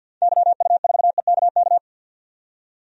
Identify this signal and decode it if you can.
Morse code, heard as a keyed tone, about 47 wpm, 700 Hz. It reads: YU4EKK
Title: DXCC-100 Call Signs - 6 Characters - 50wpm